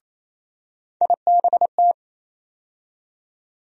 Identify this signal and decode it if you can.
Morse code, heard as a keyed tone, about 28 wpm, 700 Hz. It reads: IBT
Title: USA Call Sign Postfixes - 3 Letters 28wpm